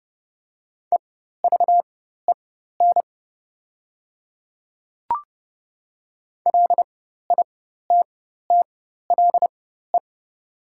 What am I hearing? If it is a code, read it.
Morse code, heard as a keyed tone, about 30 wpm, spaced as for 14 wpm, 690 Hz. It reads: EVEN LITTLE
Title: 4x Character Spacing - Top 200 Words - 30wpm